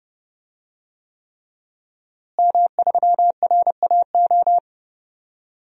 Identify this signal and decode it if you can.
Morse code, heard as a keyed tone, about 30 wpm, 700 Hz. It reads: M3RAO
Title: DXCC-100 Call Signs - 5 Characters - Repeated 3x - 30wpm